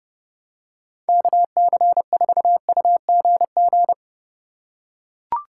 Morse code, heard as a keyed tone, about 30 wpm, 700 Hz. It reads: KC4UGG
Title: USA Call Signs - 6 Characters - 4x Character Spacing - 30wpm